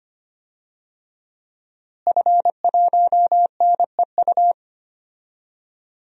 Morse code, heard as a keyed tone, about 25 wpm, 700 Hz. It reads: F1NEU
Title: DXCC-100 Call Signs - 5 Characters - Repeated 2x - 25wpm